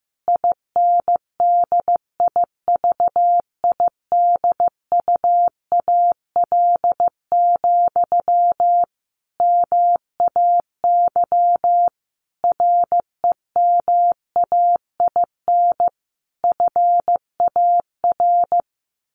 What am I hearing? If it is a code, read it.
Morse code, heard as a keyed tone, about 14 wpm, 700 Hz. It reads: INDIVIDUAL, MAY REMAIN FAR
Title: Sentences from Top 1400 Words 15wpm